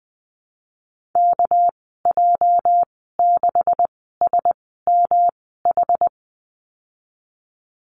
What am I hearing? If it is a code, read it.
Morse code, heard as a keyed tone, about 20 wpm, spaced as for 15 wpm, 700 Hz. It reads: KJ6SMH
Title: 245 Lesson 39g - Letter z - Call Signs 20wpm